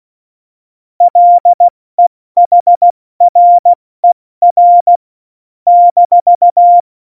Morse code, heard as a keyed tone, about 16 wpm, 700 Hz. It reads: LEHRER -
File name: Morsekurs / Witze / Witz 07. Mai 2019, 70 F80